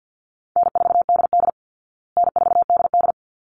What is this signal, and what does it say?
Morse code, heard as a keyed tone, about 50 wpm, 700 Hz. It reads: N4DD N4DD
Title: Call Signs Repeated 3x - 50wpm